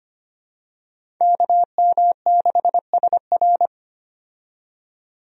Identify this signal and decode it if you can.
Morse code, heard as a keyed tone, about 25 wpm, 700 Hz. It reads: KM6SR
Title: USA Call Signs - 5 Characters - 25wpm